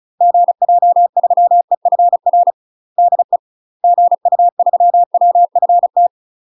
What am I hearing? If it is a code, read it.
Morse code, heard as a keyed tone, about 36 wpm, 700 Hz. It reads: GJ3EFR DE GU3WFT
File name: UK MASTER QSOs TRAINING / 35WPM / UK MASTER QSOs 0018